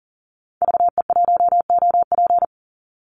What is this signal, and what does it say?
Morse code, heard as a keyed tone, about 38 wpm, 700 Hz. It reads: VE1OP
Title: Call Signs Repeated 2x - 40wpm